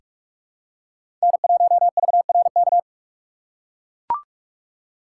Morse code, heard as a keyed tone, about 44 wpm, 700 Hz. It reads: N1VRK
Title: USA Call Signs - 6x Character Spacing - 45wpm